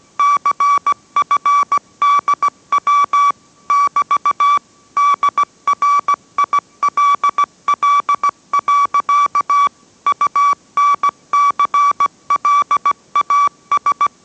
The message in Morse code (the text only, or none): CFDW <BT> DRILL. UNCLAS